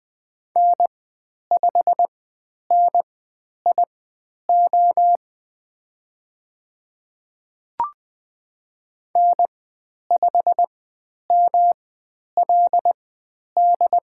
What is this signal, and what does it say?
Morse code, heard as a keyed tone, about 20 wpm, spaced as for 10 wpm, 700 Hz. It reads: N5NIO N5MLD
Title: 109 Lesson 18g - Letter m - Call Signs 20wpm